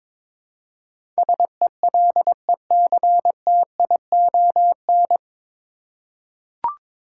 Morse code, heard as a keyed tone, about 22 wpm, 700 Hz. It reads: SELECTION
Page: Morse Code Ninja practice 22wpm